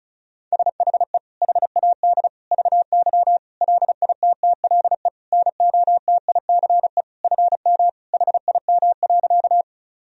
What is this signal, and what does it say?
Morse code, heard as a keyed tone, about 36 wpm, 700 Hz. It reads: SHE HAD VY LITTLE NOTICE FM HIM.
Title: Sentences from Top 400 Words and Abbreviations 35wpm